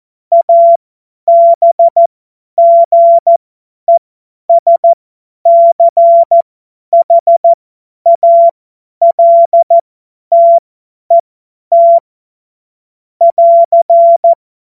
Morse code, heard as a keyed tone, about 14 wpm, spaced as for 10 wpm, 680 Hz. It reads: ABGESCHALTET <AR>
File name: Morsekurs / Witze / Witz 07. Februar 2023 50 F70